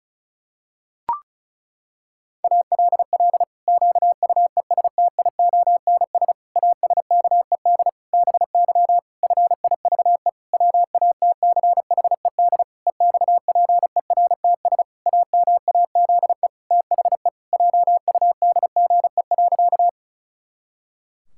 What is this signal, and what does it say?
Morse code, heard as a keyed tone, about 36 wpm, 700 Hz. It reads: ALL QUESTIONS ASKED BY FIVE WATCHED EXPERTS AMAZE THE JUDGE.
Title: Pangram Challenge v2 - 35wpm